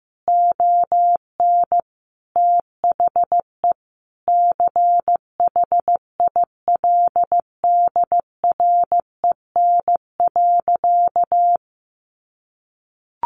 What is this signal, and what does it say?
Morse code, heard as a keyed tone, about 15 wpm, 700 Hz. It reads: ON THE CHILDREN.
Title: Sentences from Top 300 Words 15wpm